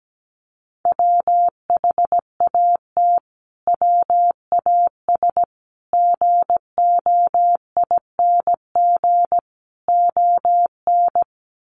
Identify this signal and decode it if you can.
Morse code, heard as a keyed tone, about 17 wpm, 700 Hz. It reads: WHAT WAS GOING ON
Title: Sets of 4 Words 17wpm